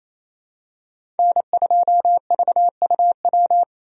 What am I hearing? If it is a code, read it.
Morse code, heard as a keyed tone, about 28 wpm, 700 Hz. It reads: N2VUW
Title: USA Call Signs - 5 Characters - Repeated 3x - 28wpm